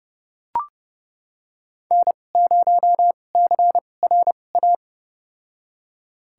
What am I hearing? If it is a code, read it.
Morse code, heard as a keyed tone, about 32 wpm, spaced as for 22 wpm, 700 Hz. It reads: N0CRA
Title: USA Call Signs - 5 Characters - 2x Character Spacing - 30wpm